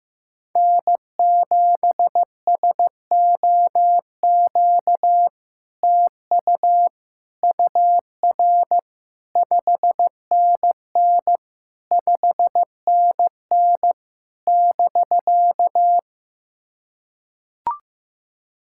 Morse code, heard as a keyed tone, about 15 wpm, 700 Hz. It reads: N7SOQ TU UR 5NN 5NN <BK>
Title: SOTA QSO Practice 15wpm